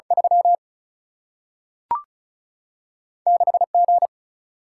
Morse code, heard as a keyed tone, about 36 wpm, 700 Hz. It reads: 3 6 G
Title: Single-Letter Single-Number 35wpm